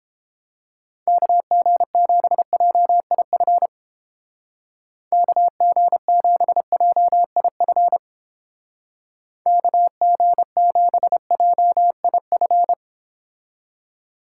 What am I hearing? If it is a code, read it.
Morse code, heard as a keyed tone, about 31 wpm, 700 Hz. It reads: KG7JIF KG7JIF KG7JIF